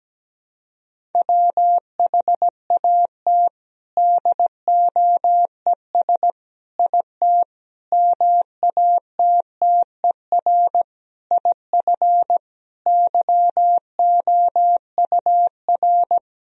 Morse code, heard as a keyed tone, about 17 wpm, 700 Hz. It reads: WHAT DOES IT MATTER IF YOUR
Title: Sentences from Top 1000 Words - Review Part 1 - 17wpm